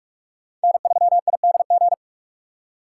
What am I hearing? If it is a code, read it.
Morse code, heard as a keyed tone, about 63 wpm, spaced as for 54 wpm, 700 Hz. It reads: N3IDG
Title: Speed Racing - USA 1x3 Call Signs 45wpm